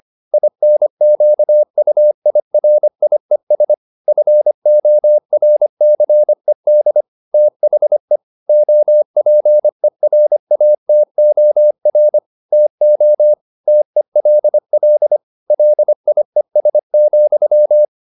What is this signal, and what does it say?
Morse code, heard as a keyed tone, about 25 wpm, 600 Hz. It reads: INQUIRIES FORCED THE OPERATOR TO TELL LIES,